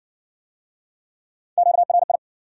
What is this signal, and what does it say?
Morse code, heard as a keyed tone, about 55 wpm, 700 Hz. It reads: QRI